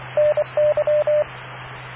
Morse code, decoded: NY